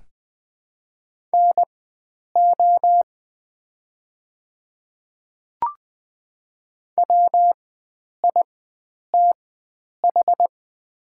Morse code, heard as a keyed tone, about 21 wpm, spaced as for 9 wpm, 700 Hz. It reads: NO WITH
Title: 4x Character Spacing - Top 100 Words - 20wpm